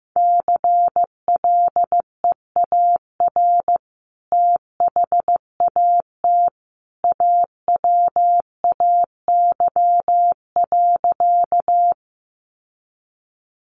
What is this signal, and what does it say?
Morse code, heard as a keyed tone, about 15 wpm, 700 Hz. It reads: CLEAR THAT AWAY.
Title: Sentences from Top 600 Words 15wpm